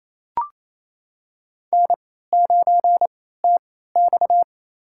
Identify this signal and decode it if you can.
Morse code, heard as a keyed tone, about 28 wpm, spaced as for 16 wpm, 700 Hz. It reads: N9TX
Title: USA Call Signs - 3x Character Spacing - 28wpm